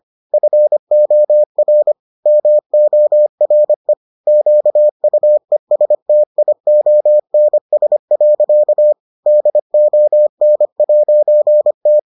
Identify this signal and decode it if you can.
Morse code, heard as a keyed tone, about 25 wpm, 600 Hz. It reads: FOR MORE QUESTIONS. DON'T